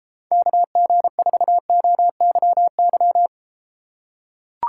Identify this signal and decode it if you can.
Morse code, heard as a keyed tone, about 34 wpm, 700 Hz. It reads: KG4OYY